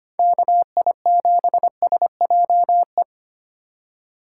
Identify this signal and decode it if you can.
Morse code, heard as a keyed tone, about 25 wpm, 700 Hz. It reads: KI7SJE
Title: USA Call Signs - 6 Characters - 4x Character Spacing - 25wpm